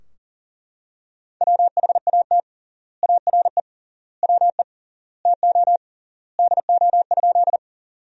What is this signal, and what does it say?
Morse code, heard as a keyed tone, about 40 wpm, 700 Hz. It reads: WHAT ARE WE TO DO?
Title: 3x Word-Spacing - Sentences from Top 100 Words - 40wpm